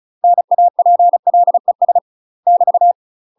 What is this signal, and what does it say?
Morse code, heard as a keyed tone, about 35 wpm, 700 Hz. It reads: NAPLES <BT>
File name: UK MASTER QSOs TRAINING / 35WPM / UK MASTER QSOs 0013